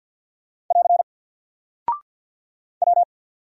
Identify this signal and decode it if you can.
Morse code, heard as a keyed tone, about 48 wpm, 700 Hz. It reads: <AR> W
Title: Single Character (Letter:Number:Punctuation:Prosign) 50wpm